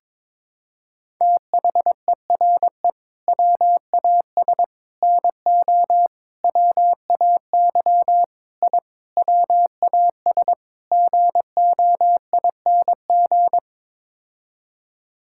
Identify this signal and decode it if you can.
Morse code, heard as a keyed tone, about 22 wpm, 700 Hz. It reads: THERE WAS NO WAY I WAS GOING
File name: practice / 22wpm / Sets of 7 Words - 22wpm